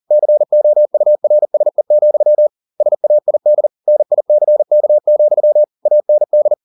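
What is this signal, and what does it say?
Morse code, heard as a keyed tone, about 40 wpm, 600 Hz. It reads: COURSE, SAID NICK, AND
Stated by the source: Morse code at 40 wpm, practice text, CW med Gnister.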